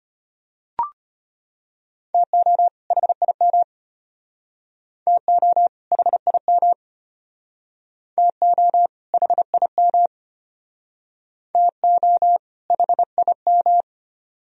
Morse code, heard as a keyed tone, about 33 wpm, 700 Hz. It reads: TO HIM TO HIM TO HIM TO HIM